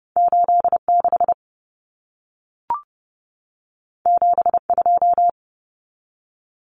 Morse code, heard as a keyed tone, about 30 wpm, 700 Hz. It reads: 86 72
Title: Common Cut Numbers - 2-Digit 30wpm